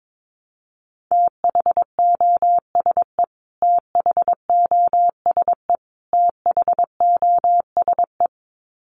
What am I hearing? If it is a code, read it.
Morse code, heard as a keyed tone, about 22 wpm, 700 Hz. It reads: THOSE THOSE THOSE
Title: Top 500 Words Repeated 22wpm